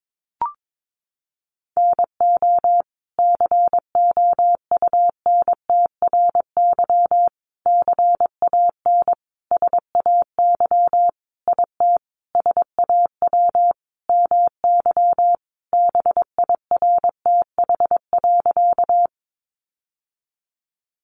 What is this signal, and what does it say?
Morse code, heard as a keyed tone, about 22 wpm, 700 Hz. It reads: NO COUNTRY CAN SAY IT SAW MY BIRTH.
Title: Sentences from Top 800 Words 22wpm